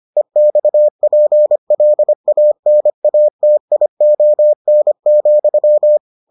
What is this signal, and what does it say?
Morse code, heard as a keyed tone, about 25 wpm, 600 Hz. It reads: EXPLANATION,